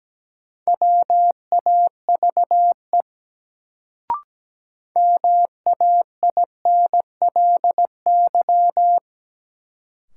Morse code, heard as a keyed tone, about 17 wpm, 700 Hz. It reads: WAVE MAINLY
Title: Sentences from Top 1400 Words - New Word Drill - 17wpm